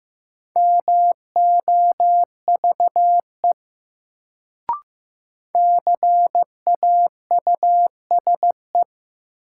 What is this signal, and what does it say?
Morse code, heard as a keyed tone, about 15 wpm, 700 Hz. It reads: MOVE CAUSE
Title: Sentences from Top 200 Words - New Word Drill - 15wpm